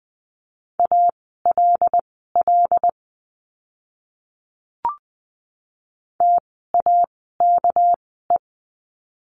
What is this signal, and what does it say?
Morse code, heard as a keyed tone, about 20 wpm, spaced as for 14 wpm, 700 Hz. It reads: ALL TAKE
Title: 199 Lesson 32f - Letter k - Words 20wpm